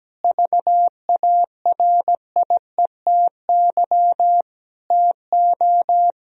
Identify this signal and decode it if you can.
Morse code, heard as a keyed tone, about 17 wpm, 700 Hz. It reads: VARIETY TO
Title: Sentences from Top 1000 Words - Review Part 4 - 17wpm